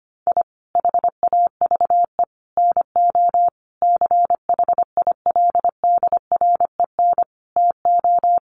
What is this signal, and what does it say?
Morse code, heard as a keyed tone, about 25 wpm, 700 Hz. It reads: I HAVE NO CHILDREN TO
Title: Sentences from Top 400 Words 25wpm